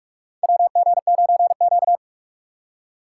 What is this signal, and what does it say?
Morse code, heard as a keyed tone, about 46 wpm, 700 Hz. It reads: WG9Q
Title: USA Call Signs - 4 Characters - 6x Character Spacing - 45wpm